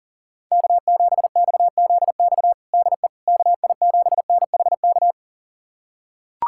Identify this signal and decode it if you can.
Morse code, heard as a keyed tone, about 37 wpm, 700 Hz. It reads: K7XZX DE KI7NHK
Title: USA Callsign DE Callsign 40wpm